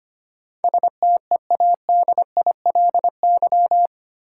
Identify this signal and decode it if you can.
Morse code, heard as a keyed tone, about 25 wpm, 700 Hz. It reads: STEADILY
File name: Eight Letter Words - 25wpm